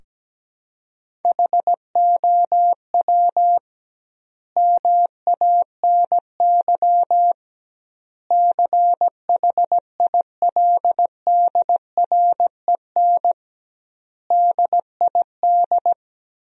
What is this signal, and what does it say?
Morse code, heard as a keyed tone, about 17 wpm, 700 Hz. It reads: HOW MANY CHILDREN DID
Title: 2x Word-Spacing - Sentences from Top 400 Words - 17wpm